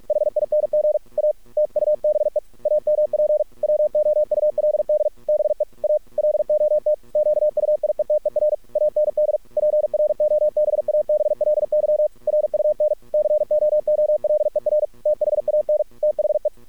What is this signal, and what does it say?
Morse code, heard as a keyed tone, about 46 wpm, 600 Hz. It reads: HING A TUBE AMP WOULD BE A LOT QUIETER AND PROBABLY RUN COOLER THAN THE
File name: archived / CRSnet